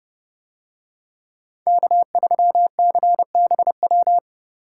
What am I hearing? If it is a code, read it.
Morse code, heard as a keyed tone, about 29 wpm, 700 Hz. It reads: K3CBW